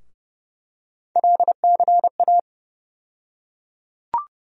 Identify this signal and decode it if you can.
Morse code, heard as a keyed tone, about 30 wpm, 700 Hz. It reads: LCA
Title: USA Call Sign Postfixes - 3 Letters 30wpm